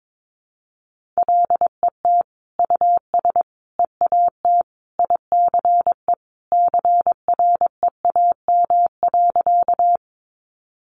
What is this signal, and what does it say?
Morse code, heard as a keyed tone, about 22 wpm, 700 Hz. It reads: LET US EAT ICE CREAM.
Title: Sentences from Top 1500 Words 22wpm